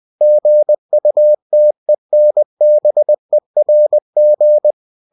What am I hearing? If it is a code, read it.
Morse code, heard as a keyed tone, about 20 wpm, 600 Hz. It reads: GUTENBERG
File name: cw-books / winnie-the-pooh / winnie-20wpm-page-0112